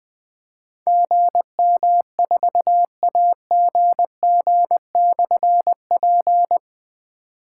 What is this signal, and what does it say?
Morse code, heard as a keyed tone, about 21 wpm, 700 Hz. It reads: GM4AGG/P
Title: Difficult Call Signs 20wpm